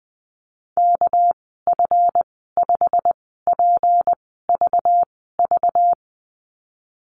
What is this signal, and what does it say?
Morse code, heard as a keyed tone, about 20 wpm, spaced as for 15 wpm, 700 Hz. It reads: KF5PVV
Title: 200 Lesson 32g - Letter k - Call Signs 20wpm